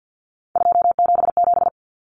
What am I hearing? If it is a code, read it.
Morse code, heard as a keyed tone, about 50 wpm, 700 Hz. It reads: 277
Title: Common Cut Numbers - 3-Digit 50wpm